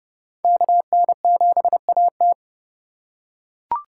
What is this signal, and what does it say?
Morse code, heard as a keyed tone, about 30 wpm, 700 Hz. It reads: KN7AT
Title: USA Call Signs - 5 Characters - 3x Character Spacing - 30wpm